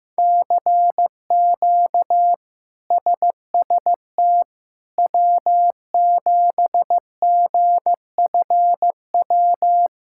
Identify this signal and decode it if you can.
Morse code, heard as a keyed tone, about 16 wpm, 700 Hz. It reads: CQ SST W7GFW
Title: K1USN SST QSO Practice - 15wpm